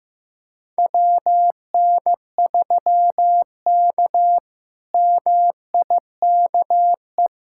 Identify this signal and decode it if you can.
Morse code, heard as a keyed tone, about 16 wpm, 700 Hz. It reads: WN3K MIKE
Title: K1USN SST QSO Practice - 15wpm